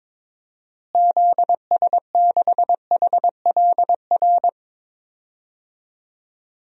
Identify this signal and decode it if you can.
Morse code, heard as a keyed tone, about 22 wpm, 700 Hz. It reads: ZS6HLR
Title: DXCC-100 Call Signs - 6 Characters - Repeated 3x - 22wpm